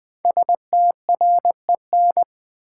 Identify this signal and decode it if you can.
Morse code, heard as a keyed tone, about 20 wpm, 700 Hz. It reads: STREN
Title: Build A Word - 5 Letters At A Time - 10-Letter Words - 20wpm